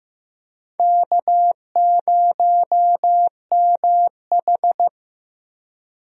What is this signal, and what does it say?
Morse code, heard as a keyed tone, about 15 wpm, 700 Hz. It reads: K0MH
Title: USA Call Signs - 4 Characters - Repeated 3x - 15wpm